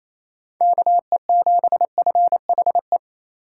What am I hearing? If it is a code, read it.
Morse code, heard as a keyed tone, about 27 wpm, 700 Hz. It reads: KE7FHE
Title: USA Call Signs - 6 Characters - 4x Character Spacing - 28wpm